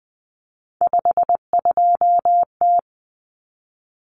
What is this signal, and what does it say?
Morse code, heard as a keyed tone, about 20 wpm, 700 Hz. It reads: 52T
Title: Common Cut Numbers - 3-Digit 20wpm